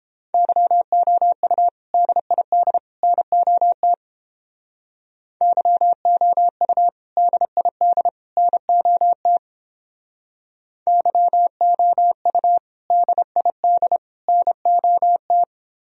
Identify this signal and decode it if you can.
Morse code, heard as a keyed tone, about 31 wpm, 700 Hz. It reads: YOU DID NOT YOU DID NOT YOU DID NOT